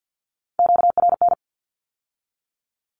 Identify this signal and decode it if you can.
Morse code, heard as a keyed tone, about 50 wpm, 700 Hz. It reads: QRN
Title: Q-Codes with Spoken Meaning - 50wpm